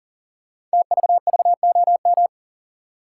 Morse code, heard as a keyed tone, about 43 wpm, 700 Hz. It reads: TVVOM
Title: Code Groups - 5 Letters - 40wpm